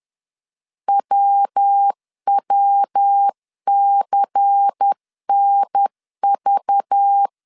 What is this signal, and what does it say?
Morse code, heard as a keyed tone, about 10 wpm, 800 Hz. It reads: WWCNV
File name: CW / DET / 1 6 1500C S C 10wpm 4252